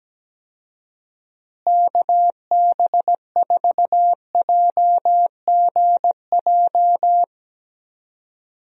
Morse code, heard as a keyed tone, about 17 wpm, 700 Hz. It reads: KB4JGJ
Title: USA Call Signs - 6 Characters - Repeated 3x - 17wpm